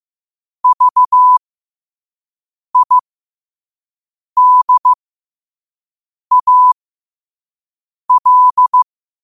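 Morse code, heard as a keyed tone, about 15 wpm, spaced as for 5 wpm, 1000 Hz. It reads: VIDAL
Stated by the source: Morse practice at 5 wpm.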